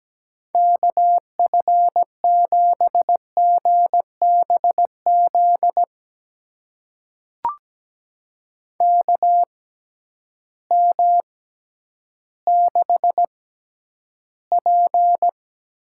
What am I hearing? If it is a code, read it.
Morse code, heard as a keyed tone, about 17 wpm, spaced as for 10 wpm, 700 Hz. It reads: KF7GBZ KM6P
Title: USA Call Signs - 6 Characters - 6x Character Spacing - 17wpm